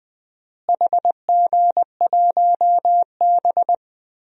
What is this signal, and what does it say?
Morse code, heard as a keyed tone, about 20 wpm, 700 Hz. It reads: HG1B
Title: DXCC-100 Call Signs - 4 Characters - Repeated 3x - 20wpm